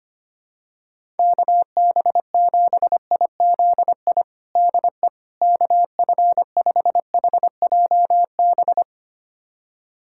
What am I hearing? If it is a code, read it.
Morse code, heard as a keyed tone, about 25 wpm, 700 Hz. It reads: KB7IZI DE KF5HJB
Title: USA Callsign DE Callsign - 6 Characters - 25wpm